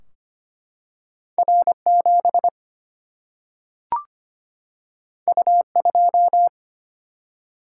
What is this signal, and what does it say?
Morse code, heard as a keyed tone, about 25 wpm, 700 Hz. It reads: R7 U2
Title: Single-Letter Single-Number 25wpm